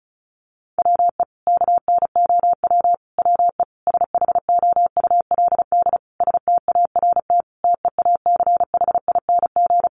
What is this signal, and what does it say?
Morse code, heard as a keyed tone, about 34 wpm, 700 Hz. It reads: WE KNOW WE SHOULD START TEACHING
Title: Sentences from Top 1100 Words 35wpm